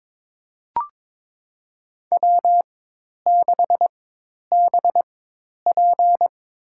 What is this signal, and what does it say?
Morse code, heard as a keyed tone, about 22 wpm, spaced as for 10 wpm, 700 Hz. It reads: W6BP
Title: USA Call Signs - 4 Characters - 4x Character Spacing - 22wpm